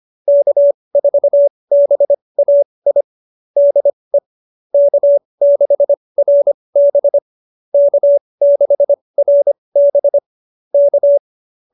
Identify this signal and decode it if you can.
Morse code, heard as a keyed tone, about 25 wpm, spaced as for 20 wpm, 590 Hz. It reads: K4BAI DE K6RB K6RB K
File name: QSO301 20